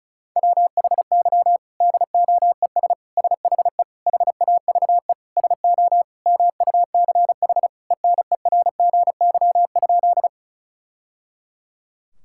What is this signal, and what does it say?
Morse code, heard as a keyed tone, about 34 wpm, 700 Hz. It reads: WHY DOES SHE HAVE SO MUCH ENERGY?
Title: Sentences from Top 1000 Words - Review Part 6 - 35wpm